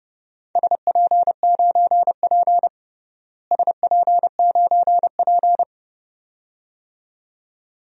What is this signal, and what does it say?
Morse code, heard as a keyed tone, about 30 wpm, 700 Hz. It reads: SP9P SP9P
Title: DXCC-100 Call Signs - 4 Characters - Repeated 2x - 30wpm